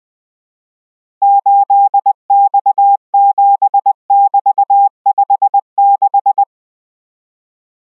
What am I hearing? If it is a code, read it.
Morse code, heard as a keyed tone, about 20 wpm, 800 Hz. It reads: 8X7<BT>56